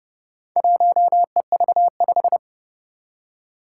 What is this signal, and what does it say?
Morse code, heard as a keyed tone, about 28 wpm, 700 Hz. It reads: 1EV5
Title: Four Challenging Letter-Number Combinations 30wpm